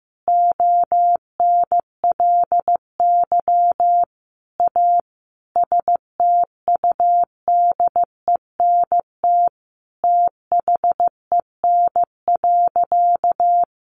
Morse code, heard as a keyed tone, about 15 wpm, 700 Hz. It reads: ONLY A STUDENT THEN.
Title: Sentences from Top 1000 Words 15wpm